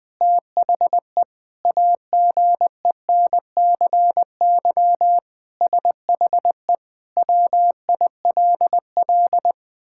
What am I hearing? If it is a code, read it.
Morse code, heard as a keyed tone, about 20 wpm, 700 Hz. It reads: THE AGENCY SHE WILL
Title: Sentences with 7-Word Phrases - Spoken Beginning - 20wpm